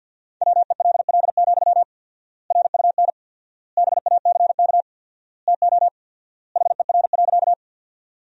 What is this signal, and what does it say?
Morse code, heard as a keyed tone, about 50 wpm, 700 Hz. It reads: WELL, RUN BACK TO HER.
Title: 4x Word-Spacing - Sentences from Top 300 Words - 50wpm